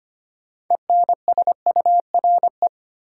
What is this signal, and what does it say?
Morse code, heard as a keyed tone, about 25 wpm, 700 Hz. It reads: ENSURE
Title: Sentences from Top 1100 Words - New Word Drill Alt - 25wpm